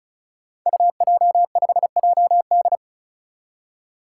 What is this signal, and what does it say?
Morse code, heard as a keyed tone, about 36 wpm, 700 Hz. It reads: UJ5JD
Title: Code Groups - 5 Letter:Numbers - 35wpm